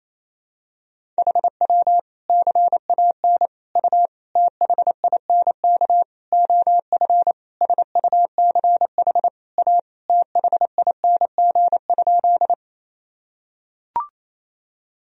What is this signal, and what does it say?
Morse code, heard as a keyed tone, about 28 wpm, 700 Hz. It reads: HW CAN U THINK OF SUCH A THING?
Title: Sentences from Top 200 Words and Abbreviations 28wpm